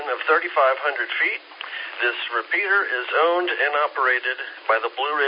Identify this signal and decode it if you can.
Emergency Alert System (SAME) tones, heard as harsh buzzing bursts, with speech. It is heard from 0.9 to 4.6 s.